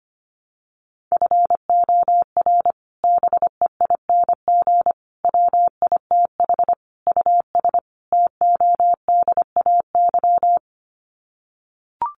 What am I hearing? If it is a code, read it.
Morse code, heard as a keyed tone, about 25 wpm, 700 Hz. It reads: FOR BEING WITH US TODAY
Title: Sets of 5 Words Encore2 25wpm